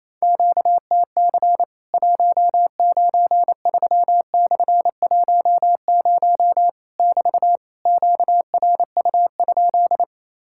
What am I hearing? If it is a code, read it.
Morse code, heard as a keyed tone, about 28 wpm, 700 Hz. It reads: QTC 193/10 <BT> QRU?
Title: Worked All Europe DX Contest - QTC Exchange - 28wpm